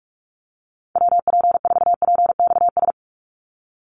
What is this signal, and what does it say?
Morse code, heard as a keyed tone, about 46 wpm, 700 Hz. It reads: WP4PXS
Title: USA Call Signs 45wpm